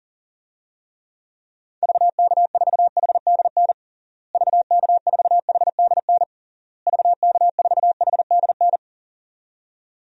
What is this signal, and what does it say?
Morse code, heard as a keyed tone, about 40 wpm, 700 Hz. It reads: VK4HDN VK4HDN VK4HDN
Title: DXCC-100 Call Signs - 6 Characters - Repeated 3x - 40wpm